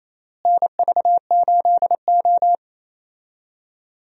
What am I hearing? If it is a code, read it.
Morse code, heard as a keyed tone, about 28 wpm, 700 Hz. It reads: NV8O
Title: USA Call Signs - 4 Characters - Repeated 3x - 28wpm